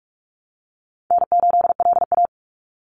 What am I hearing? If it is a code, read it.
Morse code, heard as a keyed tone, about 44 wpm, 700 Hz. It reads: N8LA
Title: Easy Call Signs Repeated 2x - 45wpm